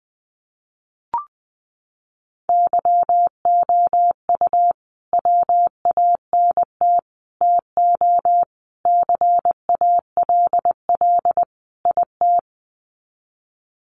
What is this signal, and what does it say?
Morse code, heard as a keyed tone, about 20 wpm, 700 Hz. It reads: YOU WANT TO CALL IT
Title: Sets of 5 Words Encore2 20wpm